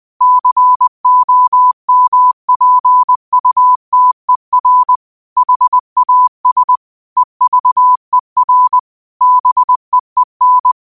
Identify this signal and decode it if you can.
Morse code, heard as a keyed tone, about 20 wpm, 1000 Hz. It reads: COMPUTER HAS EVER BEEN